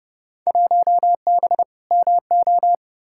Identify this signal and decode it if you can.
Morse code, heard as a keyed tone, about 30 wpm, 700 Hz. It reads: 1B MO
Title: Field Day Exchanges - 30wpm